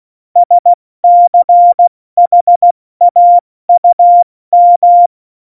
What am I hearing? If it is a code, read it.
Morse code, heard as a keyed tone, about 16 wpm, spaced as for 14 wpm, 700 Hz. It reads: SCHAUM
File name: Morsekurs / Witze / Witz 22. Oktober 2019, 70 F80